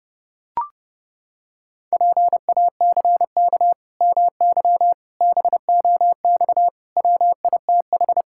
Morse code, heard as a keyed tone, about 30 wpm, 700 Hz. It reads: PACK MY BOX WITH
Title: Pangram Challenge v2 - 30wpm